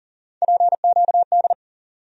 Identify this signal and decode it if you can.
Morse code, heard as a keyed tone, about 40 wpm, 700 Hz. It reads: PQD
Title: USA Call Sign Postfixes - 3 Letters 40wpm